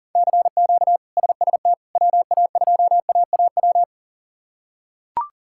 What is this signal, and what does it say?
Morse code, heard as a keyed tone, about 40 wpm, 700 Hz. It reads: CQ SST WA2AAW
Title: K1USN SST QSO Practice - 40wpm